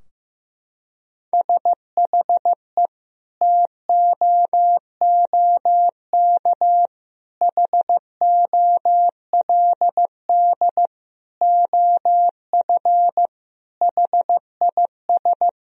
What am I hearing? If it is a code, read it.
Morse code, heard as a keyed tone, about 15 wpm, 700 Hz. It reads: SHE TOOK HOLD OF HIS
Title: Sentences from Top 1000 Words - Review Part 2 - 15wpm